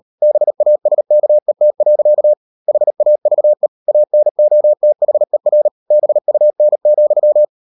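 Morse code, heard as a keyed tone, about 38 wpm, 600 Hz. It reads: BASKET. HAVE ANOTHER BUN,
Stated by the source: Morse code practice at 35–39 wpm sending English text of Ken Follett's novel Never, from CW med Gnister.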